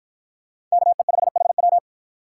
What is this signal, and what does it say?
Morse code, heard as a keyed tone, about 53 wpm, 700 Hz. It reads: KE5HW